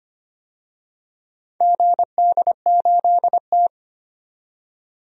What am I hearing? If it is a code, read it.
Morse code, heard as a keyed tone, about 25 wpm, 690 Hz. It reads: GD8T
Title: DXCC-100 Call Signs - 4 Characters - 25wpm